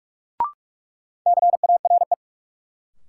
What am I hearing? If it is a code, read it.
Morse code, heard as a keyed tone, about 46 wpm, 700 Hz. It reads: CARE
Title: Sets of 3 Words - New Word Drill - 45wpm